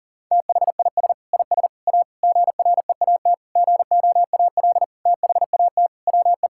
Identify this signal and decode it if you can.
Morse code, heard as a keyed tone, about 40 wpm, 700 Hz. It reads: THIS IS A GREAT GOAL THAT WE
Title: Sentences from Top 1000 Words - Review Part 6 - 40wpm